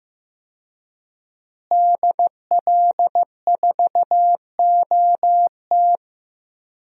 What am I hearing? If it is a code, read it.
Morse code, heard as a keyed tone, about 15 wpm, 700 Hz. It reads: DL4OT
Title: USA Call Signs - 5 Characters - Repeated 3x - 15wpm